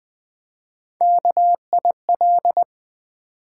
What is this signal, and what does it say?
Morse code, heard as a keyed tone, about 20 wpm, 700 Hz. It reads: KIL
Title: Build A Word - 1 Letter At A Time - 4-Letter Words - 20wpm